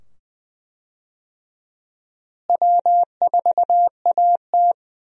Morse code, heard as a keyed tone, about 21 wpm, 700 Hz. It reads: W4AT
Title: USA Call Signs - 4 Characters - Repeated 2x - 20wpm